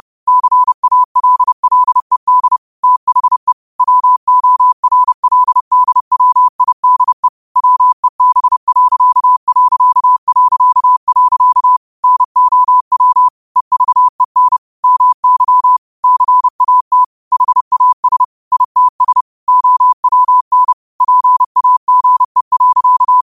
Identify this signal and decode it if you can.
Morse code, heard as a keyed tone, about 30 wpm, 1000 Hz. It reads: CALLED THE WORLDWIDE WEB.... NOW EVEN MY CAT HAS ITS OWN PAGE.
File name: qotdfiles / 30wpm